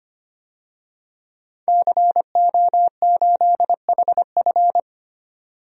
Morse code, heard as a keyed tone, about 25 wpm, 700 Hz. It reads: CO8HF
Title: DXCC-100 Call Signs - 5 Characters - 25wpm